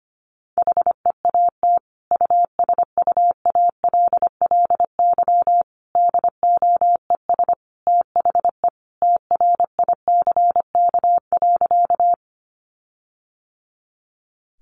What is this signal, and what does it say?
Morse code, heard as a keyed tone, about 25 wpm, 700 Hz. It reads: HEAT USUALLY DOES THE TRICK.
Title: Sentences from Top 1500 Words 25wpm